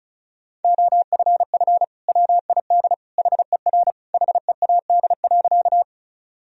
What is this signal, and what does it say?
Morse code, heard as a keyed tone, about 36 wpm, 700 Hz. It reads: OFF WID HER HEAD.
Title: Sentences from Top 300 Words and Abbreviations 35wpm